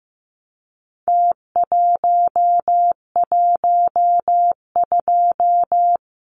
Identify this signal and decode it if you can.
Morse code, heard as a keyed tone, about 15 wpm, 700 Hz. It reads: T112
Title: Numbers - Contest Style - 4-Digit 15wpm